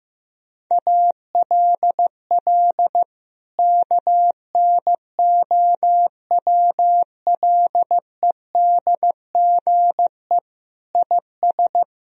Morse code, heard as a keyed tone, about 15 wpm, 700 Hz. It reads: ALL KNOWLEDGE IS
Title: Sentences from Top 1000 Words - Review Part 3 - 15wpm